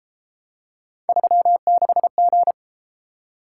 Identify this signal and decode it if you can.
Morse code, heard as a keyed tone, about 32 wpm, 700 Hz. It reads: 36G